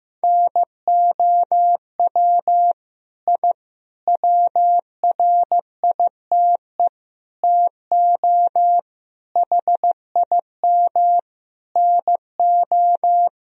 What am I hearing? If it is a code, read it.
Morse code, heard as a keyed tone, about 16 wpm, 700 Hz. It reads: NOW I WRITE TO HIM NO
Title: Sentences from Top 1000 Words - Review Part 1 - 15wpm